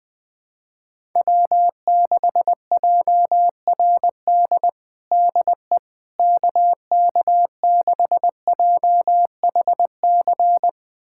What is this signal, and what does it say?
Morse code, heard as a keyed tone, about 20 wpm, 700 Hz. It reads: W6JRD DE KK6JHC
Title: USA Callsign DE Callsign 20wpm